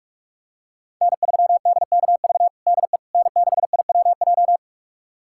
Morse code, heard as a keyed tone, about 45 wpm, 700 Hz. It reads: N3DKV DE N6IWJ
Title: USA Callsign DE Callsign - 5 Characters - 45wpm